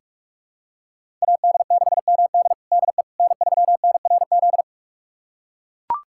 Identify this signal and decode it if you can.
Morse code, heard as a keyed tone, about 42 wpm, 700 Hz. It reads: AD6MD DE N3NRZ
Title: USA Callsign DE Callsign - 5 Characters - 45wpm